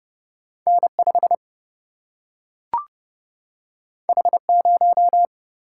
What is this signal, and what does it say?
Morse code, heard as a keyed tone, about 30 wpm, 700 Hz. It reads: N5 H0
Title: Single-Letter Single-Number 30wpm